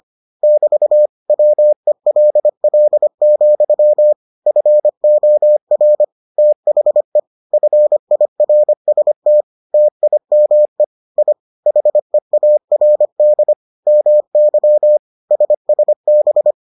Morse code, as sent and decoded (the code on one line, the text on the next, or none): -...- / .-- . .-.. .-.. --..-- / ..-. --- .-. / - .... / ..-. .. .-. ... - / - .. -- . / .. / .... .- .-. -.. / -- -.-- / ... ... -...
<BT> WELL, FOR THE FIRST TIME I HEARD MY SSB